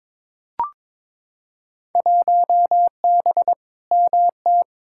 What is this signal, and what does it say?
Morse code, heard as a keyed tone, about 22 wpm, 700 Hz. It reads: 1B MT
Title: Field Day Exchanges - 22wpm